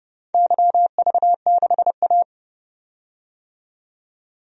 Morse code, heard as a keyed tone, about 30 wpm, 700 Hz. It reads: YV6A
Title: DXCC-100 Call Signs - 4 Characters - 30wpm